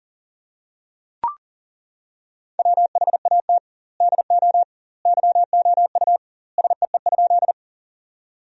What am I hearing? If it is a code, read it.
Morse code, heard as a keyed tone, about 37 wpm, 700 Hz. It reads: WHAT DO YOU SEE?
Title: 2x Word-Spacing - Sentences from Top 100 Words - 40wpm